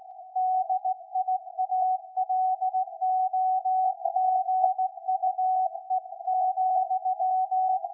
Morse code, heard as a keyed tone, about 17 wpm, 750 Hz. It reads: DIALOGUE,
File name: iz7slz / VLF / SAQ 13-02-2023